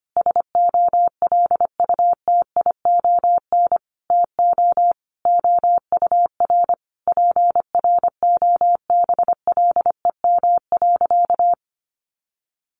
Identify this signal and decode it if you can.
Morse code, heard as a keyed tone, about 25 wpm, 700 Hz. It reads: SOLUTION TO OUR PROBLEM.
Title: Sentences from Top 1300 Words 25wpm